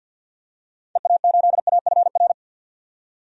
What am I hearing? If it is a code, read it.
Morse code, heard as a keyed tone, about 46 wpm, 700 Hz. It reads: EA8AFR
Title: DXCC-100 Call Signs - 6 Characters - 50wpm